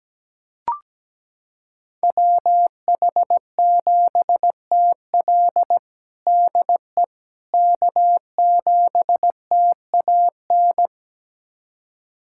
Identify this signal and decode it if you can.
Morse code, heard as a keyed tone, about 17 wpm, 700 Hz. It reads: WH7TL DE K7TAN
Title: USA Callsign DE Callsign - 5 Characters - 17wpm